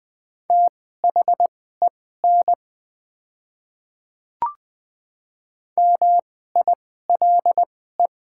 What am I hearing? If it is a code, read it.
Morse code, heard as a keyed tone, about 20 wpm, spaced as for 14 wpm, 700 Hz. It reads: THEN MILE
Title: 2x Character Spacing - Top 500 Words - 20wpm